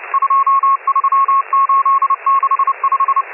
Morse code, heard as a keyed tone, about 30 wpm, 1090 Hz, with radio noise. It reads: 23865